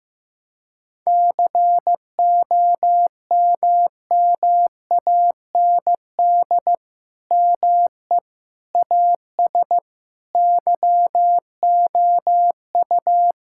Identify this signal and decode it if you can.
Morse code, heard as a keyed tone, about 16 wpm, 700 Hz. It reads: COMMAND ME AS YOU
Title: Sentences from Top 1600 Words 15wpm